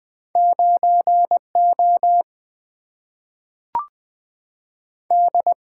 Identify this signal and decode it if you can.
Morse code, heard as a keyed tone, about 20 wpm, 700 Hz. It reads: 9O D
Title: Single-Letter Single-Number 20wpm